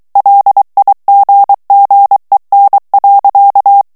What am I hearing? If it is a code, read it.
Morse code, heard as a keyed tone, about 23 wpm, 800 Hz. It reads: LIGGEN.
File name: HOLLANDAIS texte clair 1 vitesse 22 mots minute